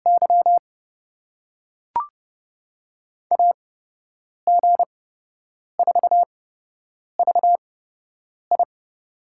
Morse code, heard as a keyed tone, about 30 wpm, spaced as for 8 wpm, 700 Hz. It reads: Y AG4VI
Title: USA Call Signs - 5 Characters - 8x Character Spacing - 30wpm